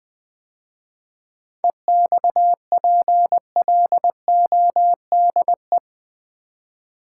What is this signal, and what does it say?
Morse code, heard as a keyed tone, about 20 wpm, 700 Hz. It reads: EXPLODE
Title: 7 Letter Words - First 3 Letters Given - 20wpm